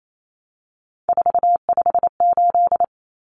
Morse code, heard as a keyed tone, about 28 wpm, 700 Hz. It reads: 458
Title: Numbers - 3-Digit 28wpm